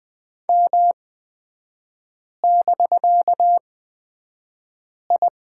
Morse code, heard as a keyed tone, about 20 wpm, spaced as for 5 wpm, 700 Hz. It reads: M<BK>I
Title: 250 Lesson 40e - BK - Three Characters 20wpm